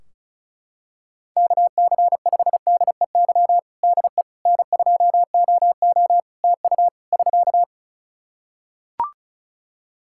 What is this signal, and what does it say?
Morse code, heard as a keyed tone, about 36 wpm, 700 Hz. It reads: KC5DEY DE N2OO TU <SK>
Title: K1USN SST QSO Practice 35wpm